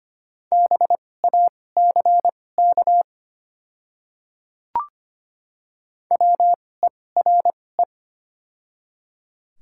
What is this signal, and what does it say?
Morse code, heard as a keyed tone, about 25 wpm, spaced as for 18 wpm, 700 Hz. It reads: BACK WERE
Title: 2x Character Spacing - Top 200 Words - 25wpm